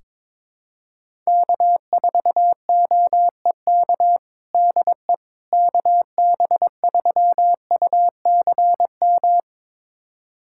Morse code, heard as a keyed tone, about 22 wpm, 700 Hz. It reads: K4OEK DE KB3UCM
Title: USA Callsign DE Callsign 22wpm